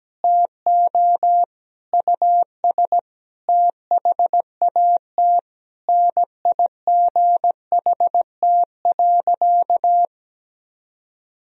Morse code, heard as a keyed tone, about 17 wpm, 700 Hz. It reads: TO US THAT NIGHT.
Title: Sentences from Top 1000 Words - Review Part 2 - 17wpm